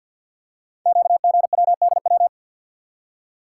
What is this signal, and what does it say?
Morse code, heard as a keyed tone, about 48 wpm, 690 Hz. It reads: QGWDW
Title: Code Groups - 5 Letters - 50wpm